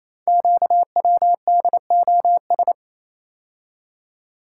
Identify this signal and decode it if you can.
Morse code, heard as a keyed tone, about 28 wpm, 700 Hz. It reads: QWDOS
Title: Code Groups - 5 Letters - 28wpm